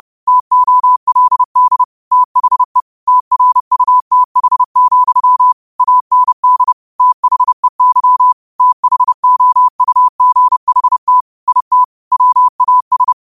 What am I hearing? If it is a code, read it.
Morse code, heard as a keyed tone, about 30 wpm, 1000 Hz. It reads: TOLD THE TRUTH, AND THEY THOUGHT IT WAS